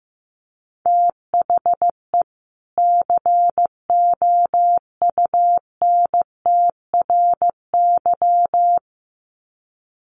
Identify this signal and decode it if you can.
Morse code, heard as a keyed tone, about 15 wpm, 700 Hz. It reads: THE COUNTRY
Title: Sets of 2 Words 15wpm